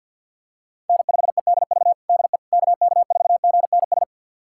Morse code, heard as a keyed tone, about 47 wpm, 700 Hz. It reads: N5EDV DE KK4GNS
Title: USA Callsign DE Callsign 50wpm